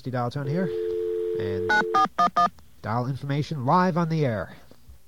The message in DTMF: 9411